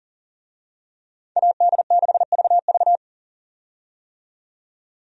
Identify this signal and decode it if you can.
Morse code, heard as a keyed tone, about 38 wpm, 700 Hz. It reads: AD6VV
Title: USA Call Signs - 5 Characters - 40wpm